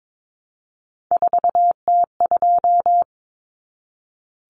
Morse code, heard as a keyed tone, about 22 wpm, 700 Hz. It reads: 4T2
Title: Common Cut Numbers - 3-Digit 22wpm